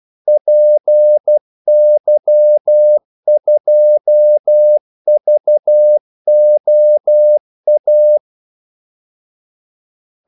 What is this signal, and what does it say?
Morse code, heard as a keyed tone, about 12 wpm, 600 Hz. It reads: PY2VOA
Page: 2 ekstra ordmellemrum mellem kaldesignalerne